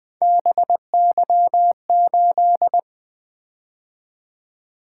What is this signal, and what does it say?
Morse code, heard as a keyed tone, about 20 wpm, 700 Hz. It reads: BY8
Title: Three Letter-Number Combinations - 20wpm